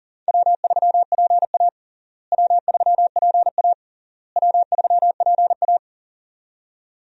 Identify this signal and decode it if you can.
Morse code, heard as a keyed tone, about 40 wpm, 700 Hz. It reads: W3PA W3PA W3PA
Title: USA Call Signs - 4 Characters - Repeated 3x - 40wpm